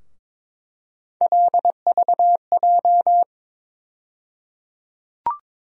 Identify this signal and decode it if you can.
Morse code, heard as a keyed tone, about 22 wpm, 700 Hz. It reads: LVJ
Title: USA Call Sign Postfixes - 3 Letters 22wpm